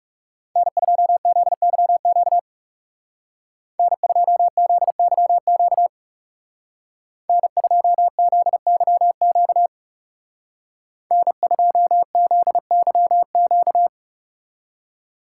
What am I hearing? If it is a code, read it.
Morse code, heard as a keyed tone, about 36 wpm, 700 Hz. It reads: N2ZYQ N2ZYQ N2ZYQ N2ZYQ
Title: Speed Racing - USA 1x3 Call Signs 30wpm